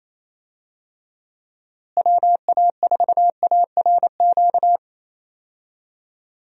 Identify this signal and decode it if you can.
Morse code, heard as a keyed tone, about 28 wpm, 700 Hz. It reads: WA4ARQ
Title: USA Call Signs - 6 Characters - Repeated 3x - 28wpm